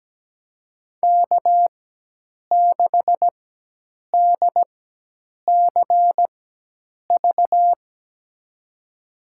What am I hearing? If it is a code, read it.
Morse code, heard as a keyed tone, about 17 wpm, spaced as for 8 wpm, 700 Hz. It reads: K6DCV
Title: USA Call Signs - 4x Character Spacing - 17wpm